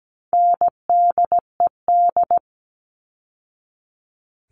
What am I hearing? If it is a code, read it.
Morse code, heard as a keyed tone, about 17 wpm, 700 Hz. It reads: NDED
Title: All Cut Numbers - 4-Digit 17wpm